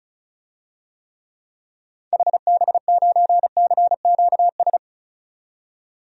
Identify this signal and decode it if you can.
Morse code, heard as a keyed tone, about 36 wpm, 700 Hz. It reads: HB9CQS
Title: DXCC-100 Call Signs - 6 Characters - 35wpm